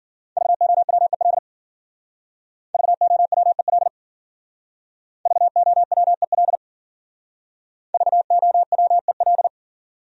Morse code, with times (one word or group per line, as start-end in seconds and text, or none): 0.4-1.4 VOWEL
2.7-3.9 VOWEL
5.3-6.6 VOWEL
7.9-9.5 VOWEL